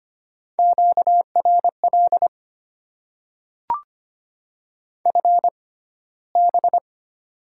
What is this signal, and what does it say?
Morse code, heard as a keyed tone, about 25 wpm, spaced as for 15 wpm, 690 Hz. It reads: QRL FB
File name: Top 90 QSO Elements - 6x Character Spacing - 25wpm